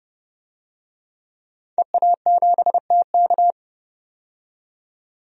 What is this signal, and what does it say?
Morse code, heard as a keyed tone, about 31 wpm, 700 Hz. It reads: EA7TK
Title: DXCC-100 Call Signs - 5 Characters - 30wpm